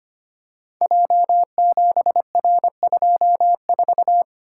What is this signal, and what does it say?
Morse code, heard as a keyed tone, about 25 wpm, 700 Hz. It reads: J7R24
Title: Code Groups - 5 Letter:Numbers - 25wpm